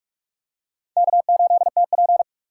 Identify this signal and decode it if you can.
Morse code, heard as a keyed tone, about 41 wpm, 690 Hz. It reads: K8TP
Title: USA Call Signs - 4 Characters - 2x Character Spacing - 45wpm